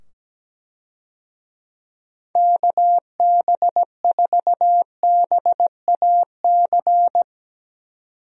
Morse code, heard as a keyed tone, about 17 wpm, 700 Hz. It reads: KB4BAC
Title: USA Call Signs - 6 Characters - 17wpm